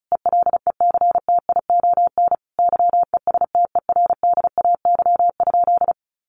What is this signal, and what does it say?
Morse code, heard as a keyed tone, about 34 wpm, 700 Hz. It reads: ELECTION YESTERDAY?
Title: A Sound Of Thunder 35wpm